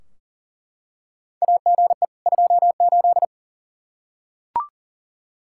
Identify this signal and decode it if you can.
Morse code, heard as a keyed tone, about 40 wpm, 700 Hz. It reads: AGE 28
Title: Practice QSO - Exchange 3 - More Info - 40wpm